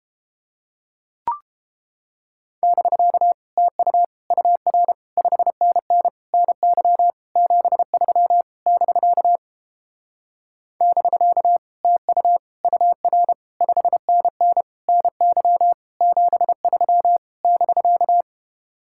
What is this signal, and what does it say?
Morse code, heard as a keyed tone, about 29 wpm, 700 Hz. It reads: <BK> TU UR 5NN NY 73 <BK> <BK> TU UR 5NN NY 73 <BK>